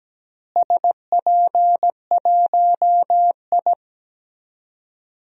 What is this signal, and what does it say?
Morse code, heard as a keyed tone, about 17 wpm, 700 Hz. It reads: SP1I
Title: DXCC-100 Call Signs - 4 Characters - Repeated 2x - 17wpm